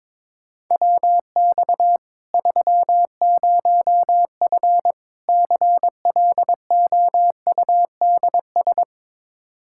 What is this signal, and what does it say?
Morse code, heard as a keyed tone, about 22 wpm, 700 Hz. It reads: WX 30F CLOUDS
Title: Practice QSO - Exchange 2 - Info - 22wpm